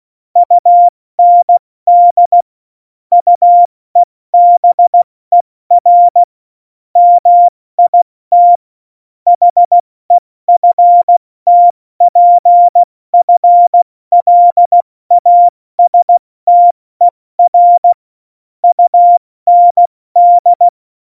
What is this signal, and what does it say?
Morse code, heard as a keyed tone, about 16 wpm, spaced as for 14 wpm, 700 Hz. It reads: UND UEBER MIT HEFTPFLASTER UND
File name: Morsekurs / Witze / Witz 20 March 2018, 70 F80